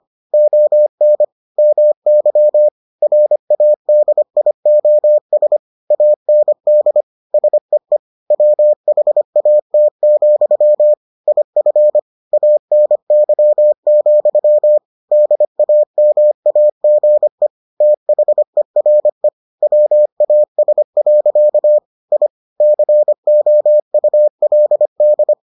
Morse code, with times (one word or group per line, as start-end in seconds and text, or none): 0.3-1.2 ON
1.6-2.7 MY
3.0-5.6 RADIOS
5.9-7.0 AND
7.3-8.0 SEE
8.3-10.9 WHAT,
11.3-12.0 IF
12.3-14.8 ANY,
15.1-17.5 DAMAGE
17.8-19.3 THERE
19.6-21.8 WAS.
22.1-22.3 I
22.6-25.3 COULD